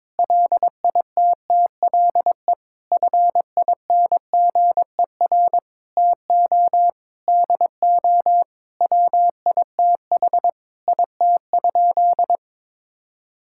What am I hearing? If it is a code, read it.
Morse code, heard as a keyed tone, about 22 wpm, 700 Hz. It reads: LITTLE FINGER TO DO WITH IT?
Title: Sentences from Top 1600 Words 22wpm